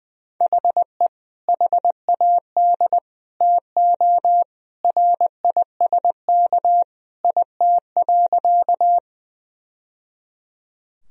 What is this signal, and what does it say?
Morse code, heard as a keyed tone, about 21 wpm, 700 Hz. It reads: HE HAD TO RISK IT.
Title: Sentences from Top 1000 Words - Review Part 4 - 20wpm